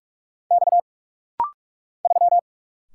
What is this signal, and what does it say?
Morse code, heard as a keyed tone, about 46 wpm, 700 Hz. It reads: X 3
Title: Single Letter-Number - Rapid-Fire 45wpm